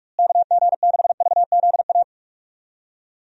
Morse code, heard as a keyed tone, about 43 wpm, 700 Hz. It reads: KG6VZA
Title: USA Call Signs - 6 Characters - Repeated 3x - 45wpm